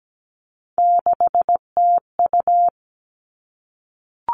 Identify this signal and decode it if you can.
Morse code, heard as a keyed tone, about 17 wpm, 700 Hz. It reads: 6TU
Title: Three Challenging Letter-Number Combinations 17wpm